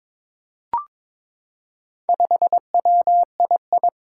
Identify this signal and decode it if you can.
Morse code, heard as a keyed tone, about 22 wpm, 700 Hz. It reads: 5WII
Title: Four Challenging Letter-Number Combinations 22wpm